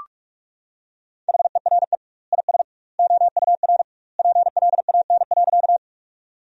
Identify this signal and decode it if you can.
Morse code, heard as a keyed tone, about 46 wpm, 700 Hz. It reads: HERE IS OUR PLAN.
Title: 2x Word-Spacing - Sentences from Top 500 Words - 45wpm